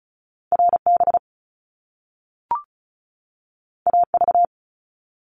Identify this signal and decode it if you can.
Morse code, heard as a keyed tone, about 36 wpm, 700 Hz. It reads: RB AV
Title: Top Two Letter Combinations in Words 35wpm